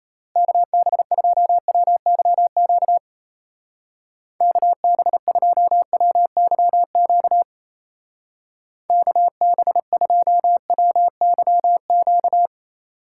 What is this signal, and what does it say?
Morse code, heard as a keyed tone, about 35 wpm, 700 Hz. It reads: KB2WYQ KB2WYQ KB2WYQ
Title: Speed Racing - USA 2x3 Call Signs 28wpm